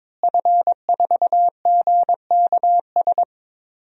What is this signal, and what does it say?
Morse code, heard as a keyed tone, about 22 wpm, 700 Hz. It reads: F4GKS
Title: DXCC-100 Call Signs - 5 Characters - Repeated 2x - 22wpm